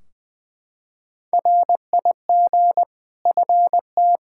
Morse code, heard as a keyed tone, about 20 wpm, 700 Hz. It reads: RIG FT
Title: Practice QSO - Exchange 2 - Info - 20wpm